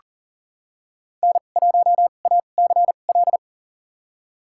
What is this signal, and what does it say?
Morse code, heard as a keyed tone, about 40 wpm, spaced as for 29 wpm, 700 Hz. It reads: N1ACL
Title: USA Call Signs - 2x Character Spacing - 40wpm